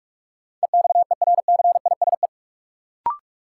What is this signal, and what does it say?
Morse code, heard as a keyed tone, about 46 wpm, 700 Hz. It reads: EXERCISE
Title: Sentences from Top 1000 Words - Review Part 4 - New Word Drill - 45wpm